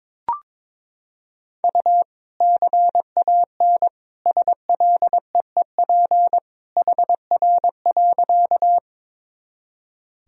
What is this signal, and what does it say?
Morse code, heard as a keyed tone, about 21 wpm, 700 Hz. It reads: U CAN SLEEP HR.
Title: Sentences from Top 500 Words and Abbreviations 22wpm